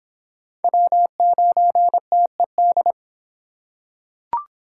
Morse code, heard as a keyed tone, about 26 wpm, 700 Hz. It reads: W9TED